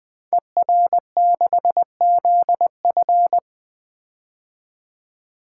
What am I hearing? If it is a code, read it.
Morse code, heard as a keyed tone, about 20 wpm, 700 Hz. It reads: ER6ZF
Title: Code Groups - 5 Letter:Numbers - 20wpm